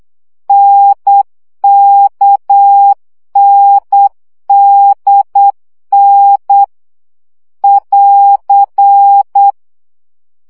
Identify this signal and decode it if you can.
Morse code, heard as a keyed tone, about 8 wpm, 800 Hz. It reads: NKNDN <AR>
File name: CW gyakorlas n d k 7 baud